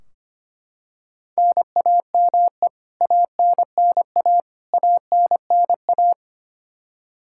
Morse code, heard as a keyed tone, about 25 wpm, 700 Hz. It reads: NAME ANNA ANNA
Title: K1USN SST QSO Practice 25wpm